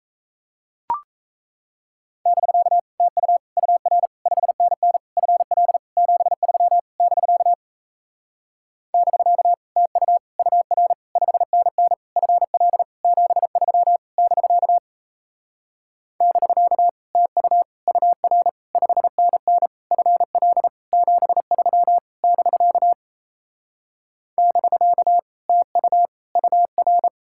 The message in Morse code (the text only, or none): <BK> TU UR 5NN FL 73 <BK> <BK> TU UR 5NN FL 73 <BK> <BK> TU UR 5NN FL 73 <BK> <BK> TU UR